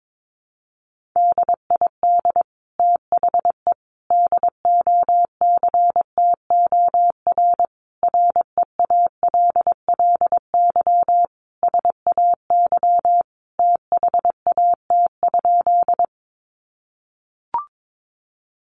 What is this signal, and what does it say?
Morse code, heard as a keyed tone, about 22 wpm, 700 Hz. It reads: DID THE DOCTOR REALLY SAY THAT?
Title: Sentences from Top 1500 Words 22wpm